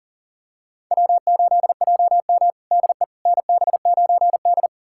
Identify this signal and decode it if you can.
Morse code, heard as a keyed tone, about 40 wpm, 700 Hz. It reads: W8JM DE NB9D
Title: USA Callsign DE Callsign - 4 Characters - 40wpm